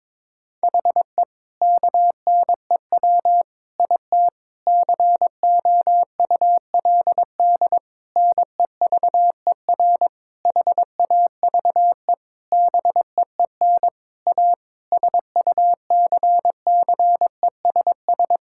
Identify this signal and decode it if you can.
Morse code, heard as a keyed tone, about 22 wpm, 700 Hz. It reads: HE KNEW IT COULD NEVER HAVE BEEN A SUCCESS
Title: Sentences from Top 1000 Words - Review Part 6 - 22wpm